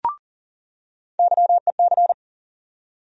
Morse code, heard as a keyed tone, about 40 wpm, 700 Hz. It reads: YEC
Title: USA Call Sign Postfixes - 3 Letters 40wpm